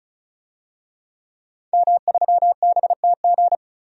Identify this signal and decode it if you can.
Morse code, heard as a keyed tone, about 34 wpm, 700 Hz. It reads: M3BTG
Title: DXCC-100 Call Signs - 5 Characters - 35wpm